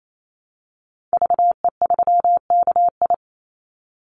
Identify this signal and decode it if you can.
Morse code, heard as a keyed tone, about 28 wpm, 700 Hz. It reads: VE3KI
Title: Call Signs Repeated 2x - 28wpm